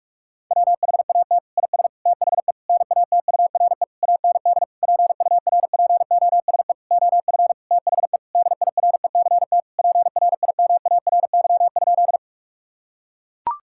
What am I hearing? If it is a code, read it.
Morse code, heard as a keyed tone, about 41 wpm, 690 Hz. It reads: WHAT IS THE NATURE AND PURPOSE OF THE DIRECT PRIMARY?
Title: Sentences from Top 1000 Words - Review Part 5 - 45wpm